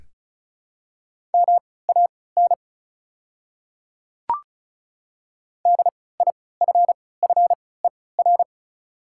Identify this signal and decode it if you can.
Morse code, heard as a keyed tone, about 35 wpm, spaced as for 20 wpm, 700 Hz. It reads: MAN DIFFER
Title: 3x Character Spacing - Top 200 Words - 35wpm